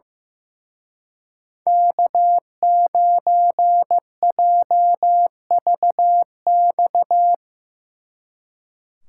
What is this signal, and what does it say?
Morse code, heard as a keyed tone, about 17 wpm, 700 Hz. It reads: K9JVX
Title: Speed Racing - USA 1x3 Call Signs 15wpm